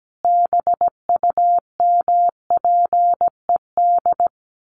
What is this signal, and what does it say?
Morse code, heard as a keyed tone, about 17 wpm, 700 Hz. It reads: BUMPED
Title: A Sound Of Thunder 17wpm